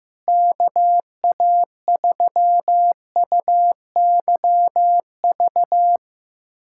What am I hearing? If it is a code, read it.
Morse code, heard as a keyed tone, about 16 wpm, 700 Hz. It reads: KA3UYV
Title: USA Callsign DE Callsign - 6 Characters - 15wpm